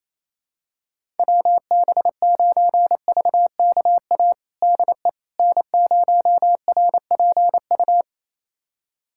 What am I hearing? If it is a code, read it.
Morse code, heard as a keyed tone, about 28 wpm, 700 Hz. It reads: WB9VKA DE N0RPU
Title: USA Callsign DE Callsign 28wpm